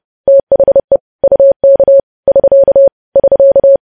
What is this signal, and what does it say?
Morse code, heard as a keyed tone, about 30 wpm, 550 Hz. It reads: THE UK <SK> <SK>
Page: news headlines in morse code at 30 wpm